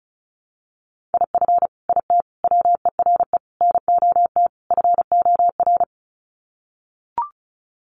Morse code, heard as a keyed tone, about 34 wpm, 700 Hz. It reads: IF IT WERE NOT FOR
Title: Sets of 5 Words 35wpm